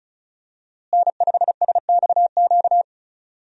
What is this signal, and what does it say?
Morse code, heard as a keyed tone, about 36 wpm, 700 Hz. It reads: N5SXQ
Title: USA Call Signs - 5 Characters - Repeated 2x - 35wpm